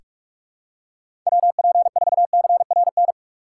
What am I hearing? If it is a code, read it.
Morse code, heard as a keyed tone, about 44 wpm, 700 Hz. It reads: WP4CRN
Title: USA Call Signs - 6 Characters - 6x Character Spacing - 45wpm